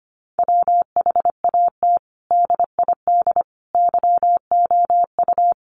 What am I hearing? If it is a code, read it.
Morse code, heard as a keyed tone, about 25 wpm, 700 Hz. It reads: WHAT DID YOU
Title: Sentences from Top 1400 Words 25wpm